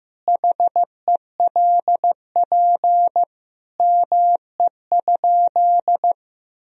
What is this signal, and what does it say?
Morse code, heard as a keyed tone, about 15 wpm, 700 Hz. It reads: HELP ME?
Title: Sentences from Top 1000 Words - Review Part 3 - 15wpm